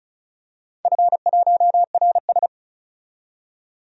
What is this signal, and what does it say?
Morse code, heard as a keyed tone, about 36 wpm, 700 Hz. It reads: F1RS
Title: DXCC-100 Call Signs - 4 Characters - 35wpm